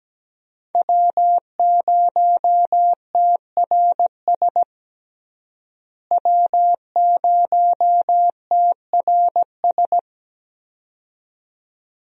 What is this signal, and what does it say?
Morse code, heard as a keyed tone, about 17 wpm, 700 Hz. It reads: W0TRS W0TRS
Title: USA Call Signs - 5 Characters - Repeated 2x - 17wpm